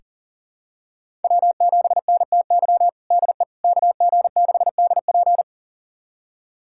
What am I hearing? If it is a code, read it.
Morse code, heard as a keyed tone, about 40 wpm, 700 Hz. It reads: W7NTY DE KG6DP
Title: USA Callsign DE Callsign - 5 Characters - 40wpm